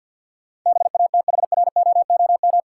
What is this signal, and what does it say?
Morse code, heard as a keyed tone, about 51 wpm, 700 Hz. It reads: BATHROOM
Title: Sets of 5 Words Encore - New Word Drill - 50wpm